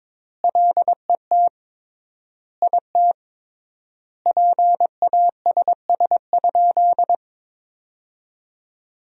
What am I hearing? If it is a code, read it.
Morse code, heard as a keyed tone, about 22 wpm, 700 Hz. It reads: LET IT PASS?
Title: 3x Word-Spacing - Sentences from Top 400 Words - 22wpm